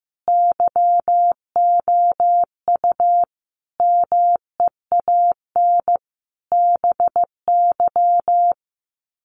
Morse code, heard as a keyed tone, about 15 wpm, 700 Hz. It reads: YOU MEAN BY
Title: Sets of 5 Words 15wpm